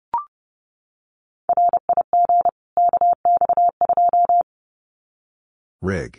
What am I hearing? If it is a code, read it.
Morse code, heard as a keyed tone, about 30 wpm, 700 Hz. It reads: RIG KX2
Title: Common QSO Phrases 30wpm